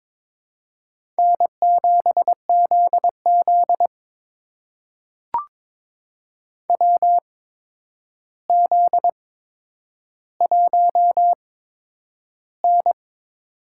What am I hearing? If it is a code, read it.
Morse code, heard as a keyed tone, about 22 wpm, spaced as for 9 wpm, 700 Hz. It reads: N7ZZ WZ1N
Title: USA Call Signs - 4 Characters - 8x Character Spacing - 22wpm